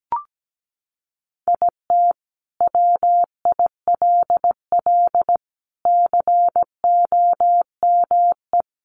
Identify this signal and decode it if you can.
Morse code, heard as a keyed tone, about 17 wpm, 700 Hz. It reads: IT WILL COME
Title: Sentences from Top 1400 Words 17wpm